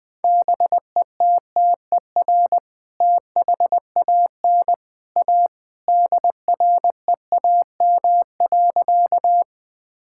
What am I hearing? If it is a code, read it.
Morse code, heard as a keyed tone, about 21 wpm, 700 Hz. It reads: BETTER THAN A DREAM.
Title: Sentences with 2-Word Phrases - Spoken Beginning - 20wpm